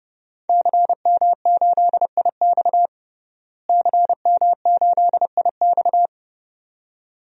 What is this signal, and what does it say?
Morse code, heard as a keyed tone, about 32 wpm, 700 Hz. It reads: CM8IX CM8IX
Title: DXCC-100 Call Signs - 5 Characters - Repeated 2x - 30wpm